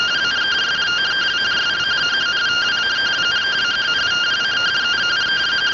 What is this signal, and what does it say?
Two-tone teleprinter signal, about 170 Hz shift.
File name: findings / MR CORONAL 55.gen-24 rtty rx